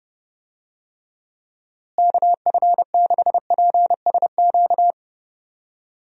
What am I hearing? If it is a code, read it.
Morse code, heard as a keyed tone, about 30 wpm, 700 Hz. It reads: KF6PSQ
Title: USA Call Signs - 6 Characters - Repeated 2x - 30wpm